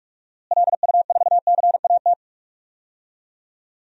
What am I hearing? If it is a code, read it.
Morse code, heard as a keyed tone, about 46 wpm, 700 Hz. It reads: LU4CAT
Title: DXCC-100 Call Signs - 6 Characters - Repeated 2x - 45wpm